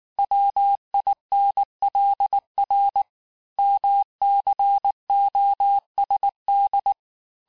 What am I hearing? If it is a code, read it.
Morse code, heard as a keyed tone, about 18 wpm, 800 Hz. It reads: WINLR MCOSD
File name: cours-du-lundi files / cours-1595-teste-CHIFFRE-vitesse18-MPM